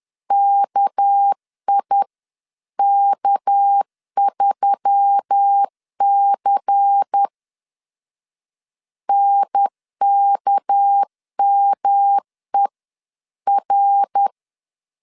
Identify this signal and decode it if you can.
Morse code, heard as a keyed tone, about 10 wpm, spaced as for 9 wpm, 800 Hz. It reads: KIK3C NKMER
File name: LEZIONI / 6 PKX0 / 6 PK 10wpm 2436